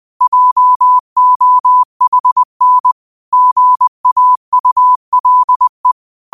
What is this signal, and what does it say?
Morse code, heard as a keyed tone, about 20 wpm, 1000 Hz. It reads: JOHN GAULE
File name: qotdfiles / 20wpm